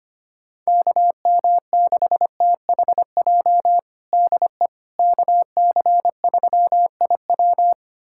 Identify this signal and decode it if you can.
Morse code, heard as a keyed tone, about 25 wpm, 700 Hz. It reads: KM6THJ DE KC3IW
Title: USA Callsign DE Callsign 25wpm